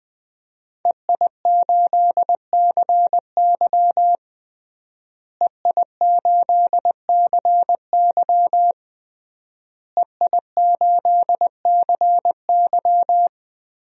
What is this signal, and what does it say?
Morse code, heard as a keyed tone, about 20 wpm, 700 Hz. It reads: EI8CY EI8CY EI8CY
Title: DXCC-100 Call Signs - 5 Characters - Repeated 3x - 20wpm